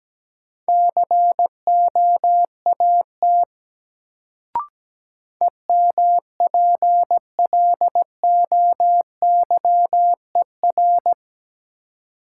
Morse code, heard as a keyed tone, about 17 wpm, 700 Hz. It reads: COAT EMPLOYER
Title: Sentences from Top 1600 Words - New Word Drill - 17wpm